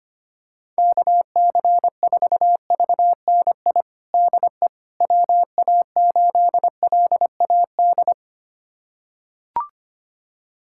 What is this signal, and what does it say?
Morse code, heard as a keyed tone, about 25 wpm, 700 Hz. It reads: KC4VNI DE WA8LAD
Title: USA Callsign DE Callsign 25wpm